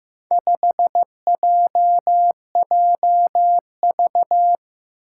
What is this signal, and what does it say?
Morse code, heard as a keyed tone, about 15 wpm, 700 Hz. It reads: 5JJV
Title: Four Challenging Letter-Number Combinations 15wpm